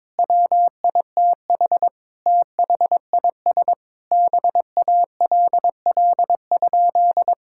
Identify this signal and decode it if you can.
Morse code, heard as a keyed tone, about 22 wpm, 700 Hz. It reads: WITH THIS BALL?
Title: Sentences from Top 1000 Words - Review Part 6 - 22wpm